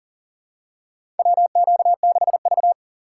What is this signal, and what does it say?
Morse code, heard as a keyed tone, about 40 wpm, 700 Hz. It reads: WQ6V
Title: USA Call Signs - 4 Characters - 4x Character Spacing - 40wpm